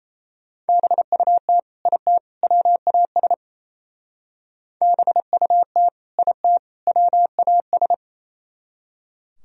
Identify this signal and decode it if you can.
Morse code, heard as a keyed tone, about 33 wpm, 700 Hz. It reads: BUT IT WAS BUT IT WAS
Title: Speed-Racing - Sets of 3 Words 28wpm